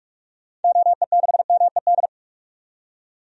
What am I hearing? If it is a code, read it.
Morse code, heard as a keyed tone, about 47 wpm, 700 Hz. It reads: OE6MED
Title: DXCC-100 Call Signs - 6 Characters - 45wpm